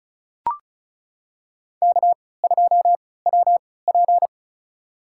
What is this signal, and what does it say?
Morse code, heard as a keyed tone, about 36 wpm, spaced as for 20 wpm, 700 Hz. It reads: K2WP
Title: USA Call Signs - 4 Characters - 3x Character Spacing - 35wpm